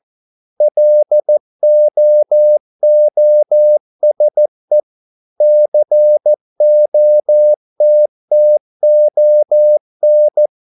Morse code, heard as a keyed tone, about 14 wpm, 600 Hz. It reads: LOOSE COTTON